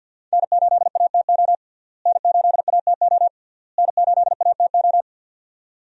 Morse code, heard as a keyed tone, about 52 wpm, 700 Hz. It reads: N8ATO N8ATO N8ATO
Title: USA Call Signs - 5 Characters - Repeated 3x - 50wpm